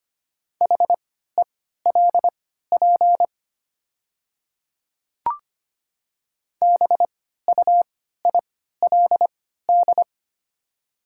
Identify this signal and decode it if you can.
Morse code, heard as a keyed tone, about 25 wpm, spaced as for 14 wpm, 690 Hz. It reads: HELP BUILD
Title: 3x Character Spacing - Top 200 Words - 25wpm